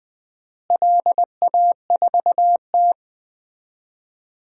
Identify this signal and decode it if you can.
Morse code, heard as a keyed tone, about 20 wpm, 700 Hz. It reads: LA4T
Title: DXCC-100 Call Signs - 4 Characters - Repeated 3x - 20wpm